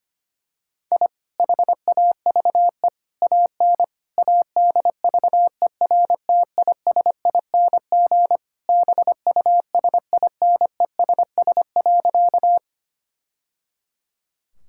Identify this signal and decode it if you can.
Morse code, heard as a keyed tone, about 25 wpm, 700 Hz. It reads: I HAVE AN ADVERTISING BUSINESS.
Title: Sentences from Top 1000 Words - Review Part 6 - 25wpm